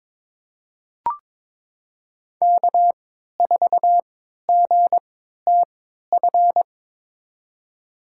Morse code, heard as a keyed tone, about 22 wpm, spaced as for 12 wpm, 700 Hz. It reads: K4GTF
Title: USA Call Signs - 3x Character Spacing - 22wpm